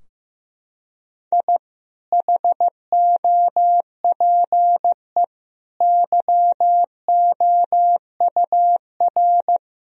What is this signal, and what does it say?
Morse code, heard as a keyed tone, about 15 wpm, 700 Hz. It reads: I HOPE YOUR
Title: Sentences from Top 1000 Words - Review Part 3 - 15wpm